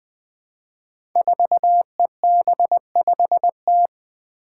Morse code, heard as a keyed tone, about 20 wpm, 700 Hz. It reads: 4EB5T
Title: Five Challenging Letter-Number Combinations 20wpm